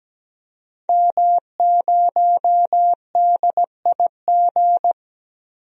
Morse code, heard as a keyed tone, about 17 wpm, 700 Hz. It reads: M0DIG
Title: DXCC-100 Call Signs - 5 Characters - 17wpm